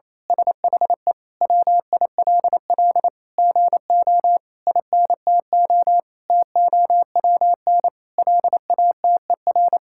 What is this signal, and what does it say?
Morse code, heard as a keyed tone, about 28 wpm, 700 Hz. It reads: SHE WILL GO INTO TOWN LATER
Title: Sentences from Top 1000 Words - Review Part 6 - 28wpm